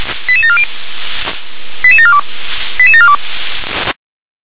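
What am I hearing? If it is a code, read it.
Five-tone selective call (ZVEI), heard as quick runs of five tones, tones of about 70 ms. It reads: 88740 88742 88742
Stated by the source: ZVEI five-tone selective call 88747 received on SDR.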